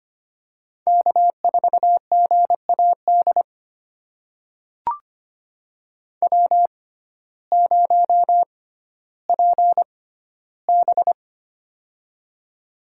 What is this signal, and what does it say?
Morse code, heard as a keyed tone, about 25 wpm, spaced as for 14 wpm, 700 Hz. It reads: K4GAD W0PB
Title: USA Call Signs - 6x Character Spacing - 25wpm